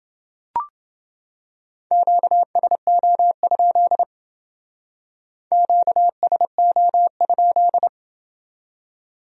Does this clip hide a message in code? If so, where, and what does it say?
Yes: Morse code from 1.9 to 7.9 s, heard as a keyed tone, about 28 wpm, 700 Hz. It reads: QSO? QSO?